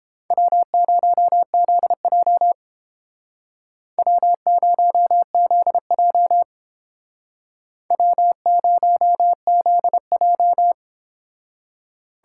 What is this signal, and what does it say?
Morse code, heard as a keyed tone, about 29 wpm, 700 Hz. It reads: W0ZJ W0ZJ W0ZJ